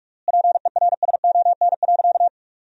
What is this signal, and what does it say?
Morse code, heard as a keyed tone, about 47 wpm, 700 Hz. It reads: PERSON.
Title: 3x Word-Spacing - Sentences from Top 500 Words - 45wpm